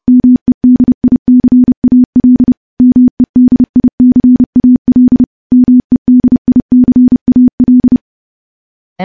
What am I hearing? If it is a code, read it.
Morse code, heard as a keyed tone, about 30 wpm, 250 Hz. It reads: MEDICAL MEDICAL MEDICAL
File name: morse-training-30WPM